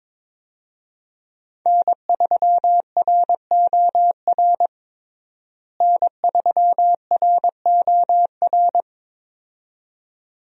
Morse code, heard as a keyed tone, about 22 wpm, 700 Hz. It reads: N3ROR N3ROR
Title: USA Call Signs - 5 Characters - Repeated 2x - 22wpm